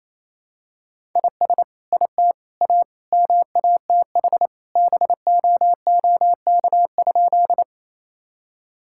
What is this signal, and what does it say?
Morse code, heard as a keyed tone, about 28 wpm, 700 Hz. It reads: IS IT A MATH BOOK?
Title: Sentences from Top 1000 Words - Review Part 6 - 28wpm